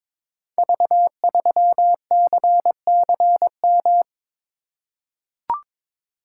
Morse code, heard as a keyed tone, about 22 wpm, 700 Hz. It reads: V3CCM
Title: Code Groups - 5 Letter:Numbers - 22wpm